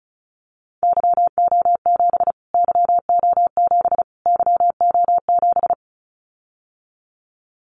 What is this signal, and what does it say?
Morse code, heard as a keyed tone, about 34 wpm, 700 Hz. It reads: YO7 YO7 YO7
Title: Callsign Prefixes for Top 25 DXCC - Repeated 3x - 35wpm